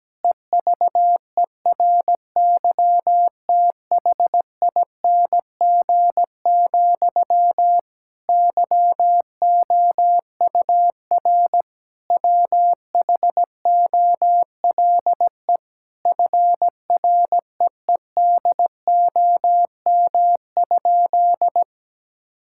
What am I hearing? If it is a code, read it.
Morse code, heard as a keyed tone, about 17 wpm, 700 Hz. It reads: EVERYTHING, YOUR WHOLE FREEDOM?
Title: Sentences from Top 1000 Words - Review Part 5 - 17wpm